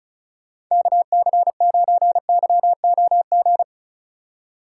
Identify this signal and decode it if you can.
Morse code, heard as a keyed tone, about 34 wpm, 700 Hz. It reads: KC9YOG
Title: USA Call Signs - 2x Character Spacing - 35wpm